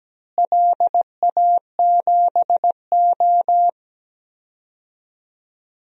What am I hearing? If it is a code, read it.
Morse code, heard as a keyed tone, about 17 wpm, 700 Hz. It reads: LA7O
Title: DXCC-100 Call Signs - 4 Characters - Repeated 3x - 17wpm